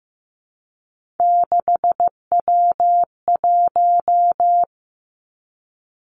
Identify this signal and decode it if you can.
Morse code, heard as a keyed tone, about 15 wpm, 700 Hz. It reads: 6W1
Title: Three Challenging Letter-Number Combinations 15wpm